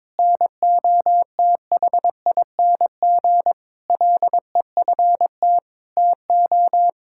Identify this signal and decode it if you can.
Morse code, heard as a keyed tone, about 21 wpm, 700 Hz. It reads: NOTHING LEFT TO
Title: Sentences from Top 1000 Words - Review Part 6 - 22wpm